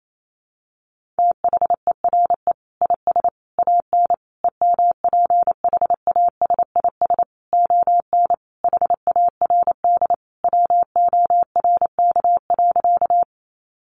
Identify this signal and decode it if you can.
Morse code, heard as a keyed tone, about 28 wpm, 700 Hz. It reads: THERE IS AN EMPHASIS ON HARD WORK.
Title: Sentences from Top 1400 Words 28wpm